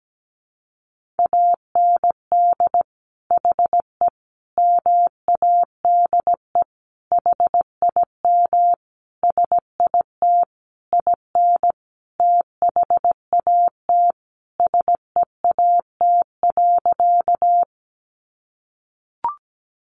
Morse code, heard as a keyed tone, about 17 wpm, 700 Hz. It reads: AND HE MADE HIM SIT IN THAT SEAT.
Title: Sentences from Top 1500 Words 17wpm